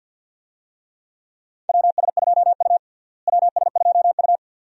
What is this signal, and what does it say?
Morse code, heard as a keyed tone, about 52 wpm, 700 Hz. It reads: WS2U WS2U
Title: USA Call Signs - 4 Characters - Repeated 2x - 50wpm